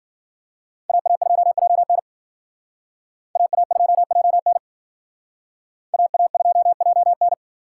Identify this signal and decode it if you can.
Morse code, heard as a keyed tone, about 56 wpm, 700 Hz. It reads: AA2JN AA2JN AA2JN